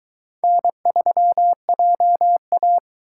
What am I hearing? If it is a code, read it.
Morse code, heard as a keyed tone, about 21 wpm, 700 Hz. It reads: N3JA